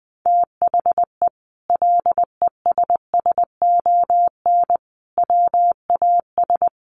Morse code, heard as a keyed tone, about 20 wpm, 700 Hz. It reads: THE LESSON WAS
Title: Sentences from Top 1200 Words 20wpm